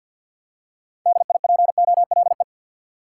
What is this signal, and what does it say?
Morse code, heard as a keyed tone, about 48 wpm, 700 Hz. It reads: DIPOLE
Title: Top 180 QSO Elements 50wpm